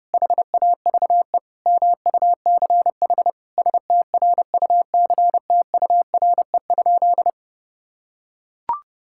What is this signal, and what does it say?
Morse code, heard as a keyed tone, about 30 wpm, 700 Hz. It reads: HAVE MUCH STRUCTURE?
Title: Sentences from Top 1000 Words - Review Part 6 - 30wpm